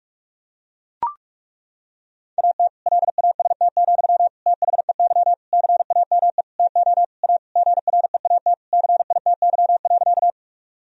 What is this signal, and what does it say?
Morse code, heard as a keyed tone, about 41 wpm, 700 Hz. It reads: AT LAST, THEY CAME TO A GREAT CITY.
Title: Sentences from Top 1000 Words - Review Part 1 - 45wpm